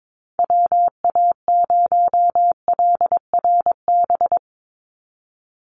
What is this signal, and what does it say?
Morse code, heard as a keyed tone, about 22 wpm, 700 Hz. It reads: WA0LRB
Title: USA Call Signs 22wpm